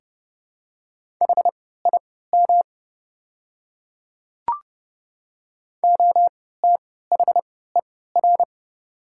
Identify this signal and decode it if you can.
Morse code, heard as a keyed tone, about 30 wpm, spaced as for 17 wpm, 700 Hz. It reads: HIM OTHER
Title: 3x Character Spacing - Top 100 Words - 30wpm